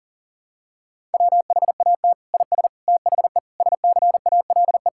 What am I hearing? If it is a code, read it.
Morse code, heard as a keyed tone, about 37 wpm, 700 Hz. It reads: WHAT IS THE SCALE
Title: Sentences from Top 1000 Words - Review Part 5 - 40wpm